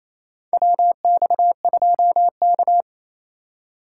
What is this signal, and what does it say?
Morse code, heard as a keyed tone, about 27 wpm, 700 Hz. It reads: WX2K
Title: USA Call Signs - 4 Characters - 4x Character Spacing - 28wpm